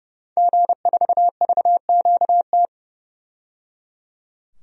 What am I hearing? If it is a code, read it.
Morse code, heard as a keyed tone, about 30 wpm, 700 Hz. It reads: G4VQT
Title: DXCC-100 Call Signs - 5 Characters - 30wpm